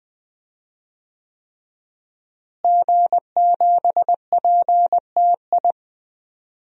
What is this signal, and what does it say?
Morse code, heard as a keyed tone, about 20 wpm, 700 Hz. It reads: G7PTI
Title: DXCC-100 Call Signs - 5 Characters - Repeated 3x - 20wpm